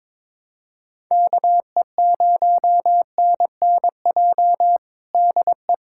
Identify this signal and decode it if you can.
Morse code, heard as a keyed tone, about 22 wpm, 700 Hz. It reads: KE0NNJ DE
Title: USA Callsign DE Callsign - 6 Characters - 22wpm